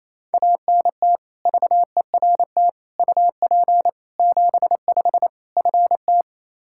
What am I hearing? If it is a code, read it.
Morse code, heard as a keyed tone, about 28 wpm, 700 Hz. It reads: ANT VERT UP 75 FT
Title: K1USN SST QSO Practice 28wpm